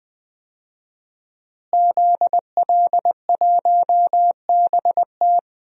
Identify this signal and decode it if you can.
Morse code, heard as a keyed tone, about 20 wpm, 700 Hz. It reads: ZL1BT
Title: DXCC-100 Call Signs - 5 Characters - Repeated 2x - 20wpm